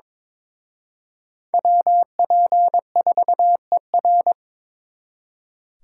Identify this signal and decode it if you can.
Morse code, heard as a keyed tone, about 26 wpm, 700 Hz. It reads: WP4ER
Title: Speed Racing - USA 2x2 Call Signs 22wpm